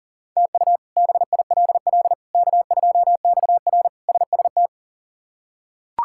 Morse code, heard as a keyed tone, about 40 wpm, 700 Hz. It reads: TU BILL K2XR SST
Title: K1USN SST QSO Practice - 40wpm